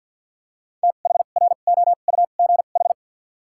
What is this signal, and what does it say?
Morse code, heard as a keyed tone, about 54 wpm, spaced as for 37 wpm, 700 Hz. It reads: THROUGH
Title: 2x Character Spacing - Top 200 Words - 50wpm